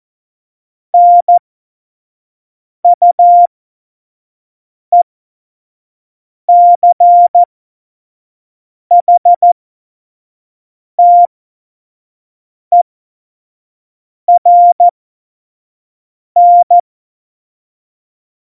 Morse code, heard as a keyed tone, about 14 wpm, spaced as for 5 wpm, 700 Hz. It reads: NUECHTERN
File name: Morsekurs / Witze / Witz 27 Juni 17, Tempo 25 F70